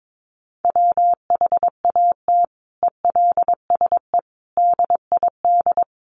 Morse code, heard as a keyed tone, about 22 wpm, 700 Hz. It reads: WHAT ELSE DID
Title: Sentences from Top 800 Words 22wpm